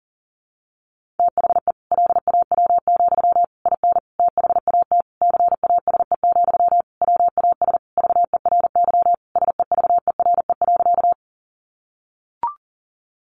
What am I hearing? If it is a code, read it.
Morse code, heard as a keyed tone, about 40 wpm, 700 Hz. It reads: THE LAW, IN THAT CASE, WAS VERY SEVERE.
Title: Sentences from Top 1400 Words 40wpm